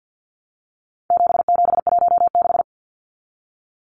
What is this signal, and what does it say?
Morse code, heard as a keyed tone, about 50 wpm, 700 Hz. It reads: 7716
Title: Numbers - 4-Digit 50wpm